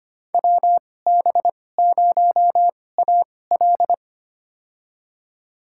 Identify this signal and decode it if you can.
Morse code, heard as a keyed tone, about 25 wpm, spaced as for 18 wpm, 700 Hz. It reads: WB0AL
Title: USA Call Signs - 5 Characters - 2x Character Spacing - 25wpm